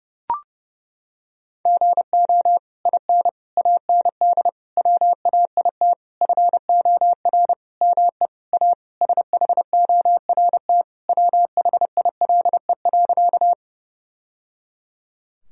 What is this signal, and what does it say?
Morse code, heard as a keyed tone, about 29 wpm, 700 Hz. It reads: GO IN AND WAIT FOR ME A SHORT WHILE.
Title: Sentences from Top 1000 Words - Review Part 2 - 30wpm